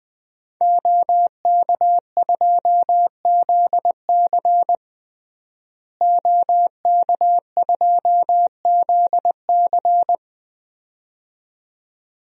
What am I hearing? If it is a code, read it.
Morse code, heard as a keyed tone, about 20 wpm, 700 Hz. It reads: OK2ZC OK2ZC
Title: DXCC-100 Call Signs - 5 Characters - Repeated 2x - 20wpm